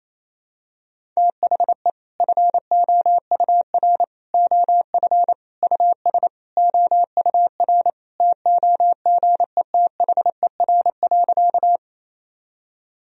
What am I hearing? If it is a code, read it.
Morse code, heard as a keyed tone, about 28 wpm, 700 Hz. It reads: THE FOUR OF US OUR TOGETHER.
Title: Sentences from Top 1000 Words - Review Part 1 - 28wpm